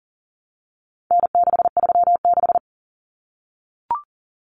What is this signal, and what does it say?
Morse code, heard as a keyed tone, about 40 wpm, 700 Hz. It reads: N636
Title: Common Cut Numbers - 4-Digit 40wpm